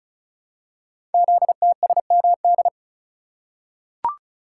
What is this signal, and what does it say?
Morse code, heard as a keyed tone, about 36 wpm, 700 Hz. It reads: ZTSMD
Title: Code Groups - 5 Letters - 35wpm